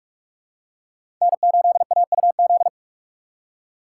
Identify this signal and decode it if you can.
Morse code, heard as a keyed tone, about 46 wpm, 700 Hz. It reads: N8AUZ
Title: USA Call Signs - 5 Characters - 2x Character Spacing - 45wpm